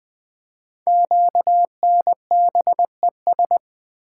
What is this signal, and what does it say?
Morse code, heard as a keyed tone, about 20 wpm, 700 Hz. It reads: QNBES
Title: Code Groups - 5 Letters - 20wpm